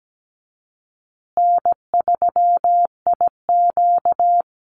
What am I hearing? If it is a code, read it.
Morse code, heard as a keyed tone, about 17 wpm, 700 Hz. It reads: N3IQ
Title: Call Signs Repeated 2x - 17wpm